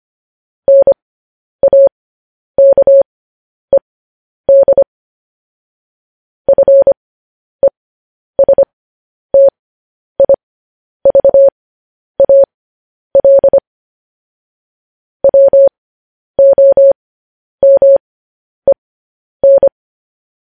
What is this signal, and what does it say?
Morse code, heard as a keyed tone, about 25 wpm, spaced as for 10 wpm, 550 Hz. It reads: NAKED FESTIVAL WOMEN